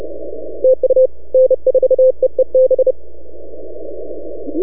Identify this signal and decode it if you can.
Morse code, heard as a keyed tone, about 30 wpm, 520 Hz. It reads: TU N4EEB
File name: NAQPCW-8-2009 / W1UJ NAQP CW 09 1800Z